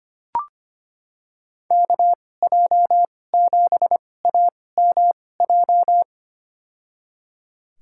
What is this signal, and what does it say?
Morse code, heard as a keyed tone, about 25 wpm, spaced as for 18 wpm, 700 Hz. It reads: KJ7AMJ
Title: USA Call Signs - 6 Characters - 2x Character Spacing - 25wpm